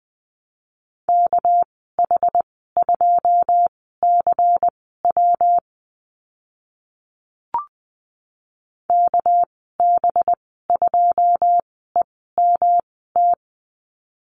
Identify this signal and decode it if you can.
Morse code, heard as a keyed tone, about 20 wpm, spaced as for 14 wpm, 700 Hz. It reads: KH2CW KB2EMT
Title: 200 Lesson 32g - Letter k - Call Signs 20wpm